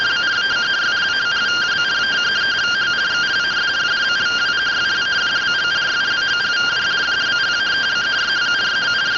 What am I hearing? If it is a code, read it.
Two-tone teleprinter signal, about 170 Hz shift.